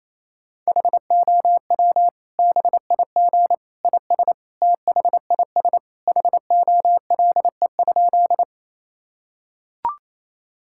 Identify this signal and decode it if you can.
Morse code, heard as a keyed tone, about 28 wpm, 700 Hz. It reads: HOW BIG IS THIS HOLE?
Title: Sentences from Top 1600 Words 28wpm